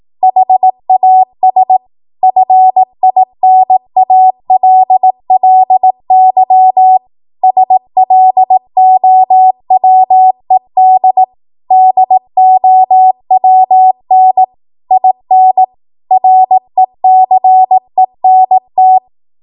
Morse code, heard as a keyed tone, about 18 wpm, 750 Hz. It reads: HAS FINALLY SLOWED DOWN IN RECENT
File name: Morse / 18WPM